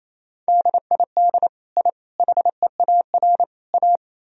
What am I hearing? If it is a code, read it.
Morse code, heard as a keyed tone, about 28 wpm, 700 Hz. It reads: DID I HEAR A
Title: Sentences from Top 1000 Words - Review Part 2 - 28wpm